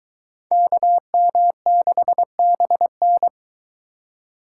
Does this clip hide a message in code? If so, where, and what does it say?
Yes: Morse code from 0.5 to 3.3 s, heard as a keyed tone, about 21 wpm, 700 Hz. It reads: KM6BN